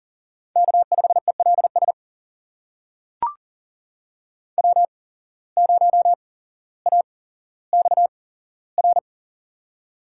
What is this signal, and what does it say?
Morse code, heard as a keyed tone, about 40 wpm, spaced as for 17 wpm, 700 Hz. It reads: K5ELS W0AXR
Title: USA Call Signs - 5 Characters - 8x Character Spacing - 40wpm